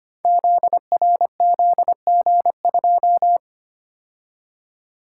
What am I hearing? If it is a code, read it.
Morse code, heard as a keyed tone, about 25 wpm, 700 Hz. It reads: ZRZG2
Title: Code Groups - 5 Letter:Numbers - 25wpm